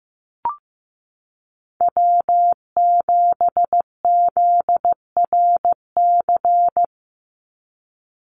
Morse code, heard as a keyed tone, about 15 wpm, 700 Hz. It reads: W7ZRC
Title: Call Signs Repeated 3x - 15wpm